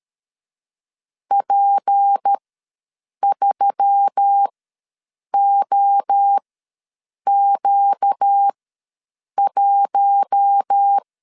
Morse code, heard as a keyed tone, about 13 wpm, spaced as for 7 wpm, 800 Hz. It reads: P3OQ1
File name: LEZIONI / 6 PKX0 / 6 P ws 12wpm 3117